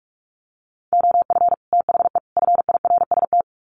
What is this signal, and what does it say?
Morse code, heard as a keyed tone, about 44 wpm, 700 Hz. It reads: OF THE FIRST
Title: Sets of 3 Words 45wpm